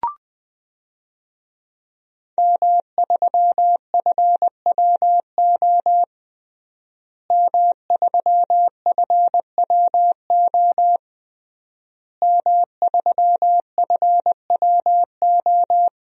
Morse code, heard as a keyed tone, about 20 wpm, 700 Hz. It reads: M3FWO M3FWO M3FWO
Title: DXCC-100 Call Signs - 5 Characters - Repeated 3x - 20wpm